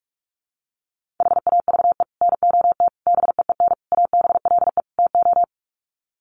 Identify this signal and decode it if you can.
Morse code, heard as a keyed tone, about 44 wpm, 700 Hz. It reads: HAVE NOT BEEN ABLE TO
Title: Sets of 5 Words 45wpm